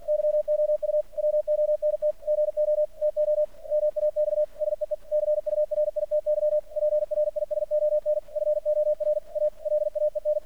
Morse code, heard as a keyed tone, about 48 wpm, 600 Hz. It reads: YOU WONT GO TO MAX SEECURITY PRISON FOR A LAEN
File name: archived / CRSnet